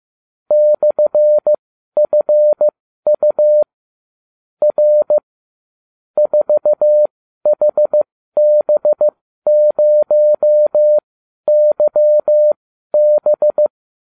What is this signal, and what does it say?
Morse code, heard as a keyed tone, about 15 wpm, spaced as for 12 wpm, 600 Hz. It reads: /FU R 4HB0YB